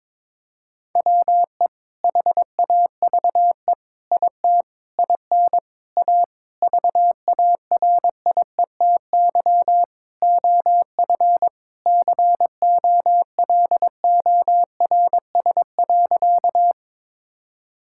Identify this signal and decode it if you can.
Morse code, heard as a keyed tone, about 22 wpm, 700 Hz. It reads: WE HAVE IT IN A VARIETY OF COLORS.
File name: practice / 22wpm / Sentences with 4-Word Phrases - 22wpm